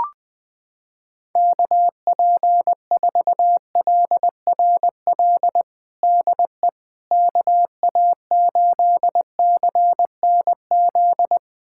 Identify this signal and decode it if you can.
Morse code, heard as a keyed tone, about 21 wpm, 700 Hz. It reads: KP4LRL DE KA8CNZ
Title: USA Callsign DE Callsign 20wpm